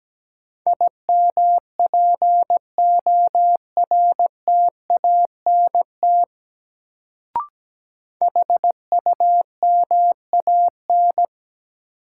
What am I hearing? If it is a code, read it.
Morse code, heard as a keyed tone, about 17 wpm, 700 Hz. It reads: IMPORTANT HUMAN
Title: Sentences from Top 600 Words - New Word Drill - 17wpm